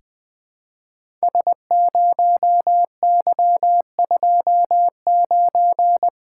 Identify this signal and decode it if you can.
Morse code, heard as a keyed tone, about 20 wpm, 700 Hz. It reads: S0Y29
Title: Code Groups - 5 Letter:Numbers - 20wpm